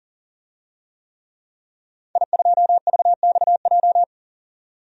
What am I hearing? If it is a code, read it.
Morse code, heard as a keyed tone, about 37 wpm, 700 Hz. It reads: I2VXJ
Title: DXCC-100 Call Signs - 5 Characters - 40wpm